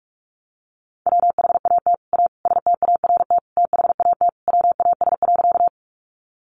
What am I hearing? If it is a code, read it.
Morse code, heard as a keyed tone, about 44 wpm, 700 Hz. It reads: WHAT A START THAT WAS.
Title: Sentences from Top 300 Words 45wpm